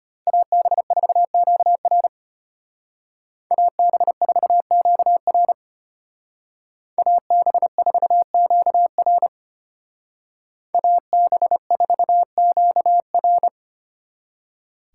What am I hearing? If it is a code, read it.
Morse code, heard as a keyed tone, about 32 wpm, 700 Hz. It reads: AB4QR AB4QR AB4QR AB4QR